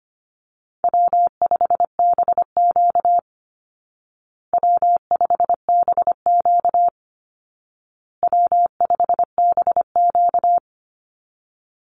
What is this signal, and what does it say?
Morse code, heard as a keyed tone, about 25 wpm, 700 Hz. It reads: W5BQ W5BQ W5BQ
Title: Easy Call Signs Repeated 3x - 25wpm